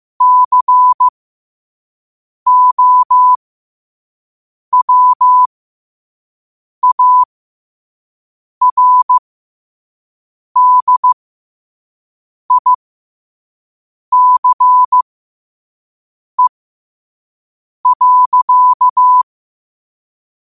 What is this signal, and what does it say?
Morse code, heard as a keyed tone, about 15 wpm, spaced as for 5 wpm, 1000 Hz. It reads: COWARDICE.
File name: qotdfiles / 5wpm